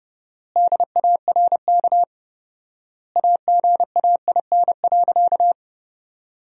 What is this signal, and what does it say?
Morse code, heard as a keyed tone, about 31 wpm, 700 Hz. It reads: DARK AGAIN.
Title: 4x Word-Spacing - Sentences from Top 500 Words - 30wpm